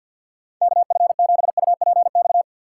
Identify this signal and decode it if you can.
Morse code, heard as a keyed tone, about 52 wpm, 700 Hz. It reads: KF7UPX
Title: USA Call Signs - 6 Characters - 3x Character Spacing - 50wpm